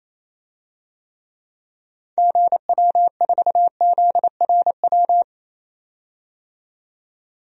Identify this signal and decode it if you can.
Morse code, heard as a keyed tone, about 28 wpm, 700 Hz. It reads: GW4ZRW
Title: DXCC-100 Call Signs - 6 Characters - 28wpm